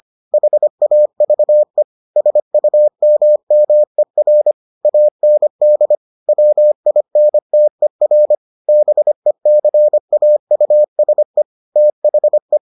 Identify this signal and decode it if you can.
Morse code, heard as a keyed tone, about 25 wpm, 600 Hz. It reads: HAVE SUMMER AND WINTER BECAUSE THE